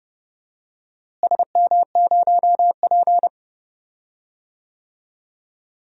Morse code, heard as a keyed tone, about 32 wpm, 700 Hz. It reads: SM0P
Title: DXCC-100 Call Signs - 4 Characters - Repeated 2x - 30wpm